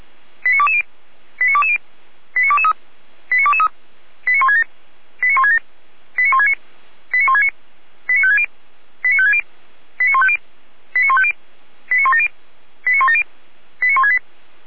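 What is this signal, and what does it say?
Five-tone selective call (ZVEI), heard as quick runs of five tones, tones of about 70 ms. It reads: 89220 89220 89223 89223 89167 89167 89169 89169 89560 89560 89150 89150 89170 89170 89168
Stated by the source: ZVEI five-tone selective call, address 89119, received by SDR.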